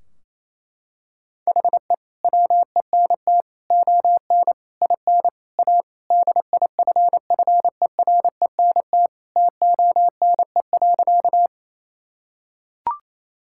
Morse code, heard as a keyed tone, about 28 wpm, 700 Hz. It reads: HE WENT ON IN A DIFFERENT TONE.
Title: Sentences from Top 1600 Words 28wpm